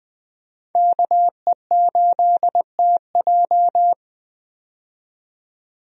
Morse code, heard as a keyed tone, about 21 wpm, 700 Hz. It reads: KE8TJ
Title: USA Call Signs - 5 Characters - 20wpm